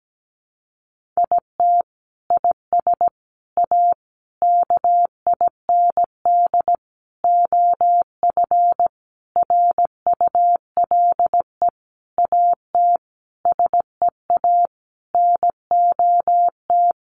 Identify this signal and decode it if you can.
Morse code, heard as a keyed tone, about 17 wpm, 700 Hz. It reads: IT IS A KIND OF RULE AT SEA NOT
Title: Sentences from Top 1500 Words 17wpm